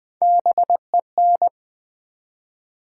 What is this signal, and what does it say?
Morse code, heard as a keyed tone, about 20 wpm, 700 Hz. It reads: BEN
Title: Build A Word - 1 Letter At A Time - 3 to 5-Letter Words - 20wpm